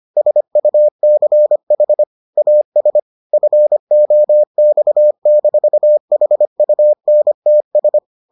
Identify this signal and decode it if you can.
Morse code, heard as a keyed tone, about 25 wpm, 600 Hz. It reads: SUCH AS FOX-HUNTS